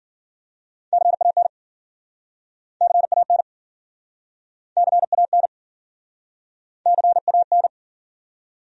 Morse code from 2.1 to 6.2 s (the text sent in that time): CAN CAN